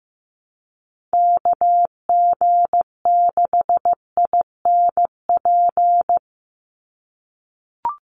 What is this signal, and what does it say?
Morse code, heard as a keyed tone, about 15 wpm, 700 Hz. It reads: KG6INP
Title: USA Call Signs 15wpm